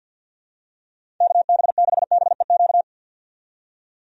Morse code, heard as a keyed tone, about 49 wpm, 700 Hz. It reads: KB6BEQ
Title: USA Call Signs - 6 Characters - 50wpm